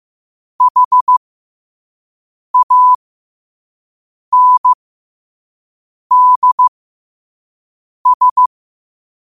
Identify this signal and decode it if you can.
Morse code, heard as a keyed tone, about 15 wpm, spaced as for 5 wpm, 1000 Hz. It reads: HANDS
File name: qotdfiles / 5wpm